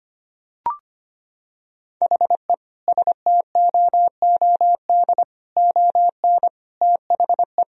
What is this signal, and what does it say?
Morse code, heard as a keyed tone, about 25 wpm, 700 Hz. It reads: HE STOOD ON THE
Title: Sentences from Top 1000 Words - Review Part 4 - 25wpm